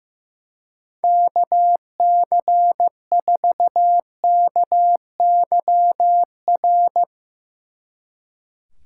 Morse code, heard as a keyed tone, about 16 wpm, 700 Hz. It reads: KC4KYR
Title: Speed Racing - USA 2x3 Call Signs 15wpm